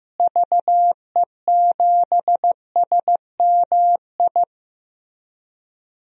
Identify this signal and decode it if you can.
Morse code, heard as a keyed tone, about 15 wpm, 700 Hz. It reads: VE7SMI
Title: DXCC-100 Call Signs - 6 Characters - Repeated 2x - 15wpm